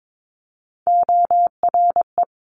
Morse code, heard as a keyed tone, about 22 wpm, 700 Hz. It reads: ORE
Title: Multi-letter Phonemes Quick Fire 22wpm